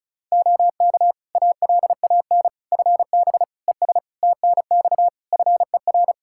Morse code, heard as a keyed tone, about 36 wpm, 700 Hz. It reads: OK ALAN FB ES TNX FER
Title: Practice QSO - Exchange 2 - Info - 35wpm